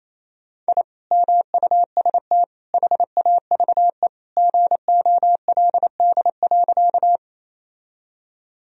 Morse code, heard as a keyed tone, about 28 wpm, 700 Hz. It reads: I MUST HAVE GOLD.
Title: Sentences from Top 1000 Words - Review Part 3 - 28wpm